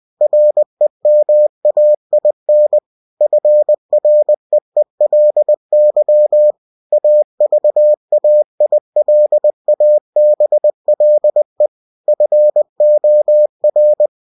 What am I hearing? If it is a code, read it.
Morse code, heard as a keyed tone, about 20 wpm, 600 Hz. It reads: REMAIN FREELY AVAILABLE FOR
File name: cw-books / winnie-the-pooh / winnie-20wpm-page-0120